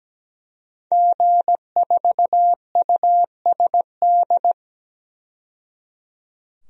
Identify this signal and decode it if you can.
Morse code, heard as a keyed tone, about 17 wpm, 700 Hz. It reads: G4USD
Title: DXCC-100 Call Signs - 5 Characters - 17wpm